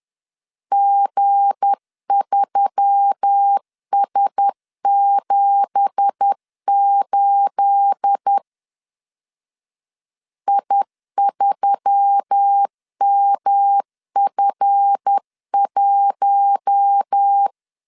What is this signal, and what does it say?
Morse code, heard as a keyed tone, about 10 wpm, 800 Hz. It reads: G3S78 I3MF1